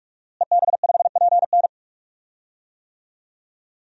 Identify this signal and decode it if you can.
Morse code, heard as a keyed tone, about 47 wpm, 700 Hz. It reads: EB5PN
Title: DXCC-100 Call Signs - 5 Characters - Repeated 2x - 45wpm